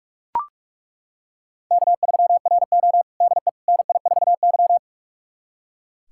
Morse code, heard as a keyed tone, about 42 wpm, 700 Hz. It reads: K3RO DE NI4Y
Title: USA Callsign DE Callsign - 4 Characters - 45wpm